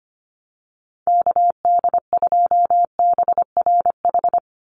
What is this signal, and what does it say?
Morse code, heard as a keyed tone, about 25 wpm, 700 Hz. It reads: KD2BRH
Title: USA Call Signs 25wpm